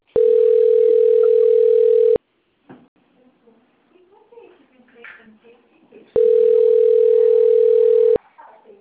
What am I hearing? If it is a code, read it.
Ringback tone (North America), 440 + 480 Hz, with speech.
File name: out- - - - - .4406